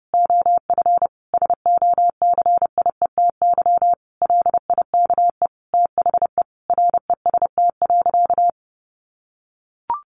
Morse code, heard as a keyed tone, about 30 wpm, 700 Hz. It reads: OF SOCIETY LIKE THE REST.
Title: Sentences from Top 1100 Words 30wpm